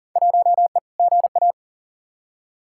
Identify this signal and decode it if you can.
Morse code, heard as a keyed tone, about 40 wpm, 700 Hz. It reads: 1E GA
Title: Field Day Exchanges - 40wpm